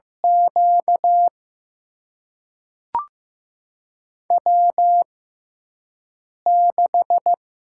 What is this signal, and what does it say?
Morse code, heard as a keyed tone, about 15 wpm, spaced as for 5 wpm, 700 Hz. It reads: Q W6
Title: USA Call Signs - 5 Characters - 6x Character Spacing - 15wpm